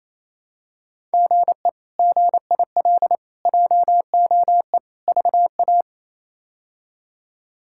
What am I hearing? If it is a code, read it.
Morse code, heard as a keyed tone, about 28 wpm, 700 Hz. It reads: GE GIL JOE VA
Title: K1USN SST QSO Practice - 28wpm